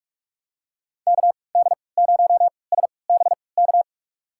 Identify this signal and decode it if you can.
Morse code, heard as a keyed tone, about 47 wpm, spaced as for 26 wpm, 700 Hz. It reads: KD0SBK
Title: USA Call Signs - 6 Characters - 3x Character Spacing - 45wpm